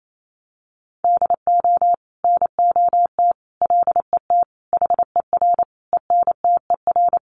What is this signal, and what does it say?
Morse code, heard as a keyed tone, about 28 wpm, 700 Hz. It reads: DO NOT LET HER ENTER
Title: Sentences from Top 1300 Words 28wpm